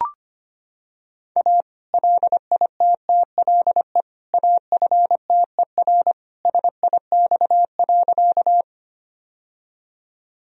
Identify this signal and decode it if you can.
Morse code, heard as a keyed tone, about 25 wpm, 700 Hz. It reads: A LITTLE AFTER SIX.
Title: Sentences from Top 1000 Words - Review Part 2 - 25wpm